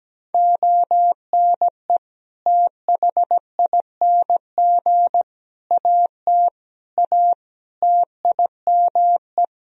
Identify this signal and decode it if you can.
Morse code, heard as a keyed tone, about 17 wpm, 700 Hz. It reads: ONE THING AT A TIME
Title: Sentences from Top 1000 Words - Review Part 2 - 17wpm